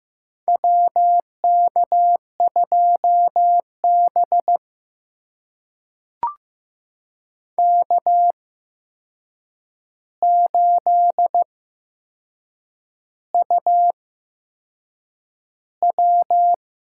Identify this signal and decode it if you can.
Morse code, heard as a keyed tone, about 15 wpm, spaced as for 6 wpm, 700 Hz. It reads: WK2B K8UW
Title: USA Call Signs - 4 Characters - 8x Character Spacing - 15wpm